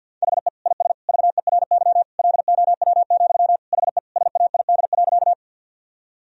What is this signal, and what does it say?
Morse code, heard as a keyed tone, about 52 wpm, 700 Hz. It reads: HE IS VERY LOW, HE SAID.
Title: Sentences from Top 1000 Words - Review Part 1 - 50wpm